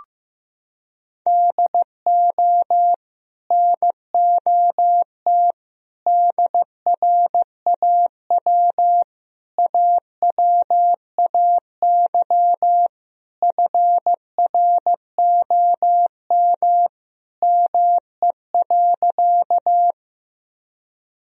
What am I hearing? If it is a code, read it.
Morse code, heard as a keyed tone, about 15 wpm, 700 Hz. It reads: DO NOT DRAW AWAY FROM ME.
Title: Sentences from Top 1000 Words - Review Part 1 - 15wpm